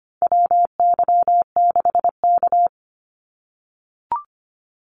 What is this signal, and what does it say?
Morse code, heard as a keyed tone, about 25 wpm, 700 Hz. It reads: WY6K
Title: Call Signs - 25wpm